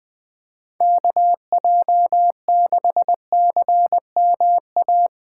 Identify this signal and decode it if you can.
Morse code, heard as a keyed tone, about 20 wpm, 700 Hz. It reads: KJ6CMA
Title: Speed Racing - USA 2x3 Call Signs 17wpm